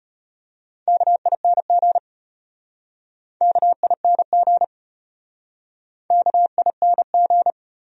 Morse code, heard as a keyed tone, about 34 wpm, 700 Hz. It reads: KING KING KING